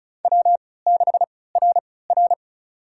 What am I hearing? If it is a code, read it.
Morse code, heard as a keyed tone, about 34 wpm, spaced as for 20 wpm, 700 Hz. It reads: W6RR
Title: USA Call Signs - 4 Characters - 3x Character Spacing - 35wpm